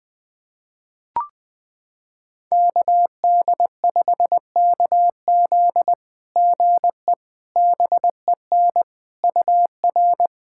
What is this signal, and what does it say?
Morse code, heard as a keyed tone, about 21 wpm, 700 Hz. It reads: KD5KZ GE BEN UR
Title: SOTA QSO Practice 20wpm